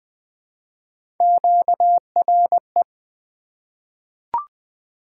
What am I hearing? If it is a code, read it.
Morse code, heard as a keyed tone, about 20 wpm, 700 Hz. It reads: QRE
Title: Three Letter-Number Combinations - 20wpm